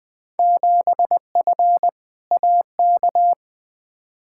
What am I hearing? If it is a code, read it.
Morse code, heard as a keyed tone, about 20 wpm, 700 Hz. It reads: 7F AK
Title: Field Day Exchanges Encore - 20wpm